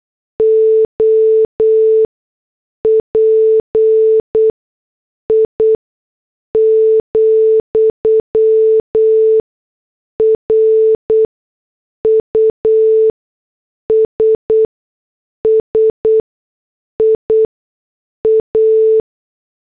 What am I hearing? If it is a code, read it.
Morse code, heard as a keyed tone, about 8 wpm, spaced as for 6 wpm, 440 Hz. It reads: OPI, RUSSIA